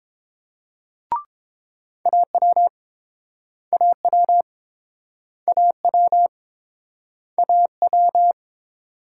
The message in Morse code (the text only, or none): AW AW AW AW